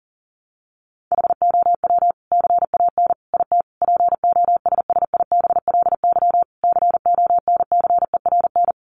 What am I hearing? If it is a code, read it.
Morse code, heard as a keyed tone, about 40 wpm, 700 Hz. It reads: HOW CAN IT POSSIBLY CONCERN
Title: Sentences from Top 1300 Words 40wpm